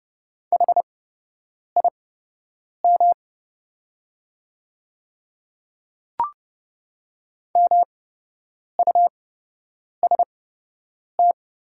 Morse code, heard as a keyed tone, about 31 wpm, spaced as for 8 wpm, 700 Hz. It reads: HIM MUST
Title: 8x Character Spacing - Top 200 Words - 30wpm